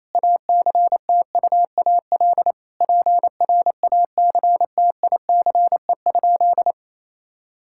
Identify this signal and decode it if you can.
Morse code, heard as a keyed tone, about 27 wpm, 700 Hz. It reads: ACTUAL PRACTICE?
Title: Sentences from Top 1000 Words - Review Part 5 - 28wpm